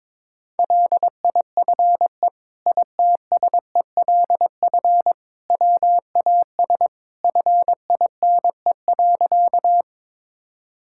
Morse code, heard as a keyed tone, about 21 wpm, 700 Hz. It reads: LIFE ITSELF WAS FINE.
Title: Sentences from Top 1000 Words - Review Part 5 - 22wpm